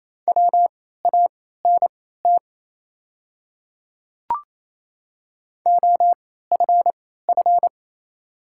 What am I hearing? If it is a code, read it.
Morse code, heard as a keyed tone, about 28 wpm, spaced as for 16 wpm, 690 Hz. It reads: WANT OFF
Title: 3x Character Spacing - Top 200 Words - 28wpm